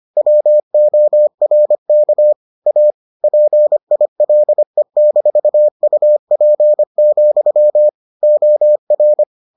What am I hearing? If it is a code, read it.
Morse code, heard as a keyed tone, about 25 wpm, 600 Hz. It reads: WORK A PILE-UP, OR